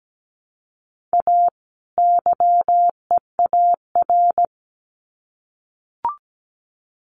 Morse code, heard as a keyed tone, about 17 wpm, 700 Hz. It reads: A YEAR
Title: Sets of 2 Words 17wpm